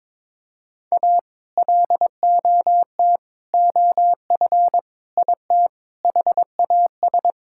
Morse code, heard as a keyed tone, about 22 wpm, 700 Hz. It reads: A LOT OF IT HAS
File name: Sets of 9 Words - 22wpm